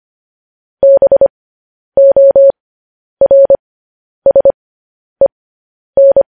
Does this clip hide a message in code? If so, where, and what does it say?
Yes: Morse code from 0.8 to 6.2 s, heard as a keyed tone, about 25 wpm, spaced as for 10 wpm, 550 Hz. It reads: BORSEN